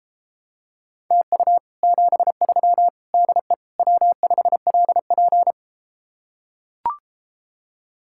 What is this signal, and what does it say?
Morse code, heard as a keyed tone, about 33 wpm, 700 Hz. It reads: TU 73 DE W5LP